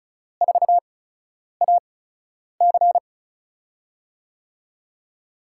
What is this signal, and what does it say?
Morse code, heard as a keyed tone, about 34 wpm, spaced as for 10 wpm, 700 Hz. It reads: 4AC
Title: USA Call Signs - 5 Characters - 8x Character Spacing - 35wpm